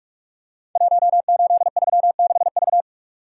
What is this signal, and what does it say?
Morse code, heard as a keyed tone, about 44 wpm, 700 Hz. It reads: 1836V
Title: Code Groups - 5 Letter:Numbers - 45wpm